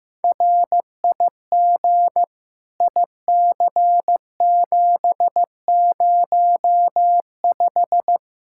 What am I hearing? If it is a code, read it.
Morse code, heard as a keyed tone, about 16 wpm, 700 Hz. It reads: RIG IC705
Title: Practice QSO - Exchange 2 - Info - 15wpm